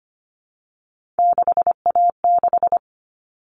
Morse code, heard as a keyed tone, about 25 wpm, 700 Hz. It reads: 6A6
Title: All Cut Numbers - 3-Digit 25wpm